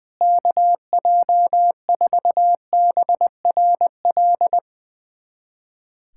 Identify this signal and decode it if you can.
Morse code, heard as a keyed tone, about 20 wpm, 700 Hz. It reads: KJ4BRL
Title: USA Callsign DE Callsign 20wpm